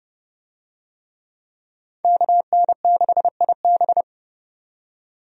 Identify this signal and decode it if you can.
Morse code, heard as a keyed tone, about 31 wpm, 700 Hz. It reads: KN6IB
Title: USA Call Signs - 5 Characters - 30wpm